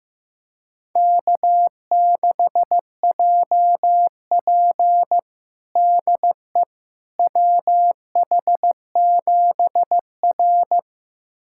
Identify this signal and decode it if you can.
Morse code, heard as a keyed tone, about 15 wpm, 700 Hz. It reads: K6JP DE WH7R
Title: USA Callsign DE Callsign - 4 Characters - 15wpm